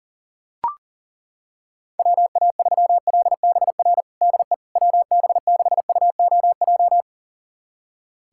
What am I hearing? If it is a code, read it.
Morse code, heard as a keyed tone, about 40 wpm, 700 Hz. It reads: WA3LBR DE WB6UOJ
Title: USA Callsign DE Callsign - 6 Characters - 40wpm